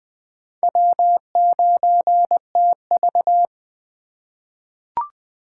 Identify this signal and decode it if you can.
Morse code, heard as a keyed tone, about 20 wpm, 700 Hz. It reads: W9TV
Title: USA Call Signs - 4 Characters - 4x Character Spacing - 20wpm